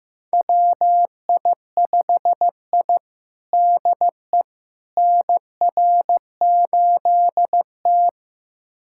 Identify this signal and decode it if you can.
Morse code, heard as a keyed tone, about 15 wpm, 700 Hz. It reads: WI5I DE NR8T
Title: USA Callsign DE Callsign - 4 Characters - 15wpm